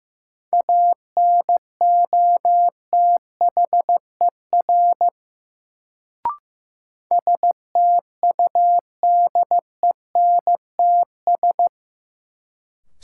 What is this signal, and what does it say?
Morse code, heard as a keyed tone, about 15 wpm, 700 Hz. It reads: ANOTHER STUDENTS
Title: Sets of 5 Words Encore - New Word Drill - 15wpm